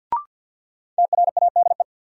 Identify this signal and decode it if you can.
Morse code, heard as a keyed tone, about 53 wpm, 700 Hz. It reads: TRADE
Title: Sets of 5 Words Encore - New Word Drill - 50wpm